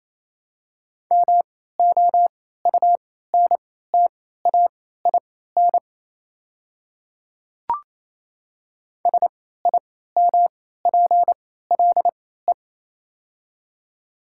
Morse code, heard as a keyed tone, about 27 wpm, spaced as for 16 wpm, 700 Hz. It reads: MOUNTAIN SIMPLE
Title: 3x Character Spacing - Top 500 Words - 28wpm